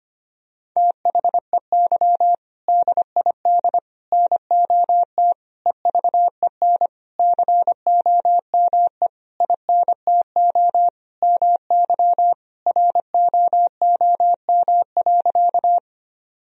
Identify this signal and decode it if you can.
Morse code, heard as a keyed tone, about 25 wpm, 700 Hz. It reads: THEY DID NOT EVEN COME INTO MY ROOM.
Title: Sentences from Top 1000 Words - Review Part 1 - 25wpm